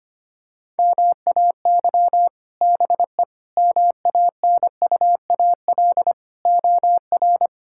Morse code, heard as a keyed tone, about 25 wpm, 690 Hz. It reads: MAY BE MANUAL OR
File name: practice / 25wpm / Difficult Words - Speed-Racing with Hints - 25wpm